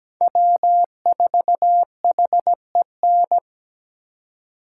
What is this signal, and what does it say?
Morse code, heard as a keyed tone, about 20 wpm, 700 Hz. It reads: W4HEN
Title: Speed Racing - USA 1x3 Call Signs 17wpm